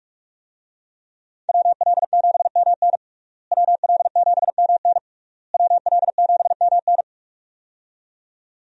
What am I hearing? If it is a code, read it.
Morse code, heard as a keyed tone, about 46 wpm, 700 Hz. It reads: WL7MN WL7MN WL7MN
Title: USA Call Signs - 5 Characters - Repeated 3x - 45wpm